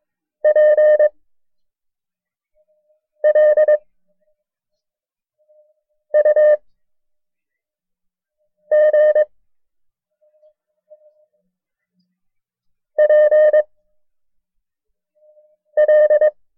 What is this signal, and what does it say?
Morse code, heard as a keyed tone, about 22 wpm, 610 Hz. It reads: P L U G P L
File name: FreeFiles / CW-practice / KB6MT-tape1-side2